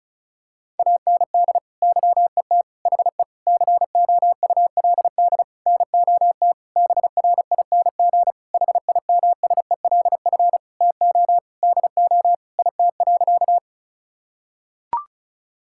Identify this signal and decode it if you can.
Morse code, heard as a keyed tone, about 36 wpm, 700 Hz. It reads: AND YET HE COULD NOT BRING HIMSELF TO DO IT.
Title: Sentences from Top 1600 Words 35wpm